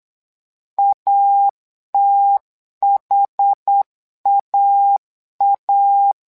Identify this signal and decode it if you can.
Morse code, heard as a keyed tone, about 8 wpm, 800 Hz. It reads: ATHAA
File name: LEZIONI / 2 UHDN / 2 UH 8wpm 2129